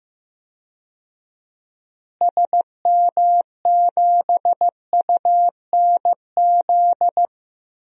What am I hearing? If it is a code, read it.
Morse code, heard as a keyed tone, about 15 wpm, 700 Hz. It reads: SM7UNZ
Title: DXCC-100 Call Signs - 6 Characters - Repeated 2x - 15wpm